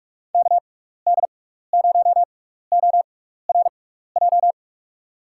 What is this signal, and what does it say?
Morse code, heard as a keyed tone, about 44 wpm, spaced as for 15 wpm, 700 Hz. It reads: KD0ORJ
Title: USA Call Signs - 6 Characters - 6x Character Spacing - 45wpm